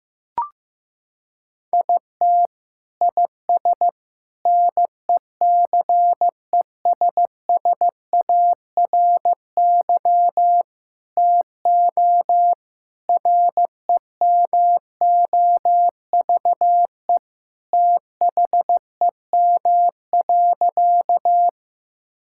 Morse code, heard as a keyed tone, about 15 wpm, 700 Hz. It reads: IT IS NECESSARY TO REMOVE THEM.
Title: Sentences from Top 1000 Words - Review Part 5 - 15wpm